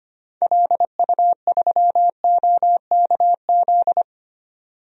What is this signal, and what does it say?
Morse code, heard as a keyed tone, about 25 wpm, 700 Hz. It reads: LU3OKZ
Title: DXCC-100 Call Signs - 6 Characters - 25wpm